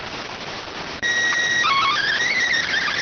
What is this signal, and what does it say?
SSTV (slow-scan television), PD 290.